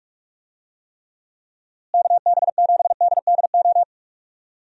Morse code, heard as a keyed tone, about 46 wpm, 700 Hz. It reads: KB7DDO
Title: USA Call Signs - 6 Characters - Repeated 2x - 45wpm